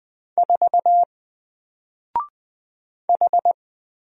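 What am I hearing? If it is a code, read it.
Morse code, heard as a keyed tone, about 20 wpm, 700 Hz. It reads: 4 H
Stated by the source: Single letter and number Morse drill at 20 wpm, from N4LCD.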